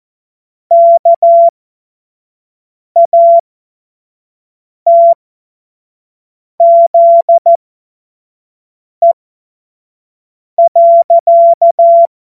Morse code, heard as a keyed tone, about 14 wpm, spaced as for 5 wpm, 680 Hz. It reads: KATZE.
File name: Morsekurs / Witze / Witz 05. April 2022 25 F70